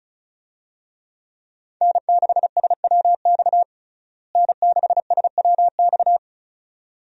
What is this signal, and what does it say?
Morse code, heard as a keyed tone, about 36 wpm, 700 Hz. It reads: N6SWX N6SWX
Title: USA Call Signs - 5 Characters - Repeated 2x - 35wpm